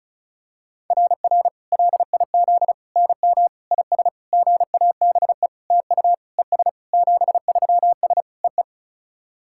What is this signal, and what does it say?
Morse code, heard as a keyed tone, about 36 wpm, 690 Hz. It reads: RR LIZ NM IS GABE TU ES 73S EE
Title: SOTA QSO Practice 35wpm